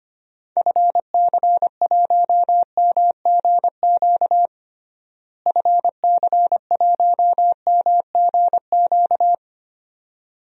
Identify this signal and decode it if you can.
Morse code, heard as a keyed tone, about 25 wpm, 700 Hz. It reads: FC1MGQ FC1MGQ
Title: DXCC-100 Call Signs - 6 Characters - Repeated 2x - 25wpm